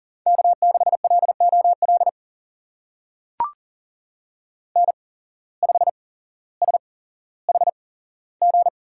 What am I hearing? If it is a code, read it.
Morse code, heard as a keyed tone, about 40 wpm, spaced as for 17 wpm, 700 Hz. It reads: K6LOL N5SHG
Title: USA Call Signs - 5 Characters - 8x Character Spacing - 40wpm